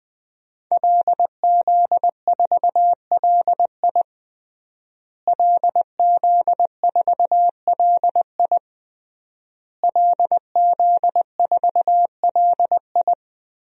Morse code, heard as a keyed tone, about 21 wpm, 700 Hz. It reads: LZ4LI LZ4LI LZ4LI
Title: DXCC-100 Call Signs - 5 Characters - Repeated 3x - 20wpm